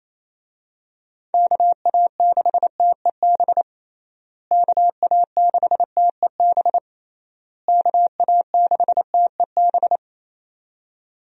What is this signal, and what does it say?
Morse code, heard as a keyed tone, about 28 wpm, 700 Hz. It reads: KA6TEB KA6TEB KA6TEB
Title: USA Call Signs - 6 Characters - Repeated 3x - 28wpm